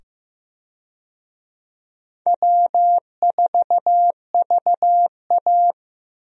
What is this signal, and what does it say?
Morse code, heard as a keyed tone, about 15 wpm, 700 Hz. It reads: W4VA
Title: USA Call Signs - 4 Characters - 15wpm